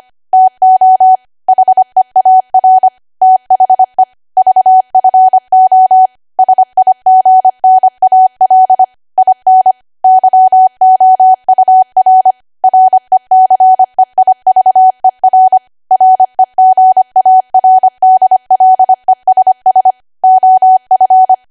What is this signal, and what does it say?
Morse code, heard as a keyed tone, about 25 wpm, 750 Hz. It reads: TO HEAR THE VFO SIGNAL IN YOUR RECEIVER REGARDLESS OF
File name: Archive / 25 WPM / 25WPM